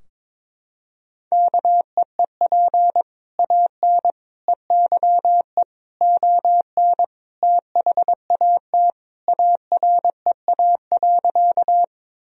Morse code, heard as a keyed tone, about 22 wpm, 700 Hz. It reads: KEEP AN EYE ON THAT AREA.
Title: Sentences from Top 1000 Words - Review Part 2 - 22wpm